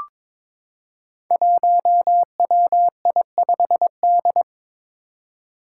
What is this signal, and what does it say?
Morse code, heard as a keyed tone, about 22 wpm, 700 Hz. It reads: 1WI5D
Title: Five Challenging Letter-Number Combinations 22wpm